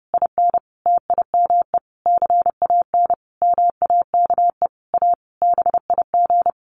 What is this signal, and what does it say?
Morse code, heard as a keyed tone, about 30 wpm, 700 Hz. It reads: IN TIME CAN MAKE A BIG
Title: A Sound Of Thunder 30wpm